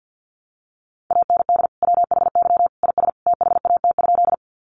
Morse code, heard as a keyed tone, about 50 wpm, 700 Hz. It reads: AND WHY IS THAT?
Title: Sentences from Top 200 Words 50wpm